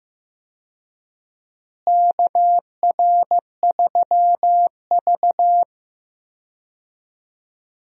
Morse code, heard as a keyed tone, about 15 wpm, 700 Hz. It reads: KR3V
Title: USA Call Signs - 4 Characters - Repeated 2x - 15wpm